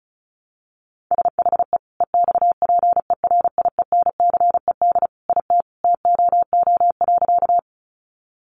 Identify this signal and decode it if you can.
Morse code, heard as a keyed tone, about 34 wpm, 700 Hz. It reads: SHE EXPERIENCED IT TOO.
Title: Sentences from Top 1100 Words 35wpm